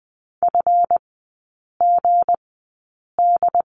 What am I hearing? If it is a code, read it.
Morse code, heard as a keyed tone, about 20 wpm, spaced as for 8 wpm, 690 Hz. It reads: FGD
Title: 159 Lesson 26e - Letter g - Three Characters 20wpm